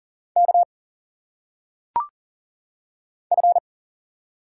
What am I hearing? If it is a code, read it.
Morse code, heard as a keyed tone, about 39 wpm, 700 Hz. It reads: K F
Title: Single Letters - Phonetic Pronunciation 40wpm